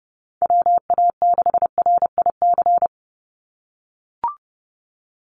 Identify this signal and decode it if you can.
Morse code, heard as a keyed tone, about 32 wpm, 700 Hz. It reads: WA6RIC
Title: NA SOTA Chaser Callsigns 30wpm